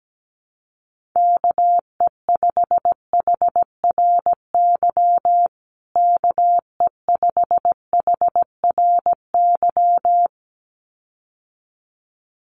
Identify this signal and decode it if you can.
Morse code, heard as a keyed tone, about 17 wpm, 700 Hz. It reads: KE5HRY KE5HRY
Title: USA Call Signs Repeated 2x 17wpm